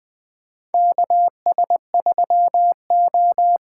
Morse code, heard as a keyed tone, about 20 wpm, 700 Hz. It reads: KS3O
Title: USA Call Signs - 4 Characters - 3x Character Spacing - 20wpm